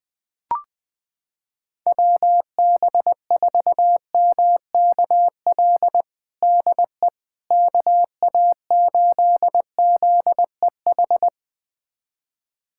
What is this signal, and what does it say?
Morse code, heard as a keyed tone, about 20 wpm, 700 Hz. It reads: WB4MKL DE KA8ZEH
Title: USA Callsign DE Callsign - 6 Characters - 20wpm